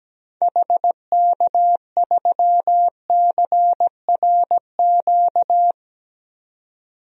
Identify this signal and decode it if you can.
Morse code, heard as a keyed tone, about 17 wpm, 700 Hz. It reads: HK3CRQ
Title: DXCC-100 Call Signs - 6 Characters - 17wpm